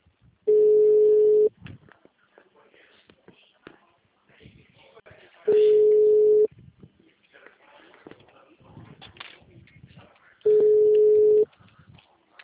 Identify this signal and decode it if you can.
Ringback tone (Europe, 425 Hz), with speech.